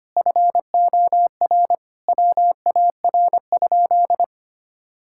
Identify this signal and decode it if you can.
Morse code, heard as a keyed tone, about 25 wpm, 700 Hz. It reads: FOR WAR?
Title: Sentences from Top 1600 Words 25wpm